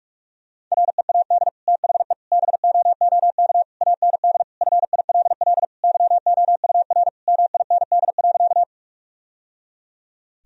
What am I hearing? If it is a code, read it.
Morse code, heard as a keyed tone, about 47 wpm, 700 Hz. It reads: READ THE BOOK AND FILL YOUR MIND.
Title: Sentences from Top 1000 Words - Review Part 3 - 45wpm